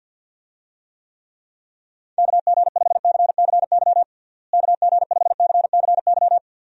Morse code, heard as a keyed tone, about 52 wpm, 700 Hz. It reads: KG5CCY KG5CCY
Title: USA Call Signs - 6 Characters - Repeated 2x - 50wpm